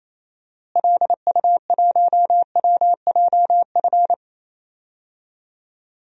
Morse code, heard as a keyed tone, about 28 wpm, 700 Hz. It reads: LU1WJF
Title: DXCC-100 Call Signs - 6 Characters - Repeated 2x - 28wpm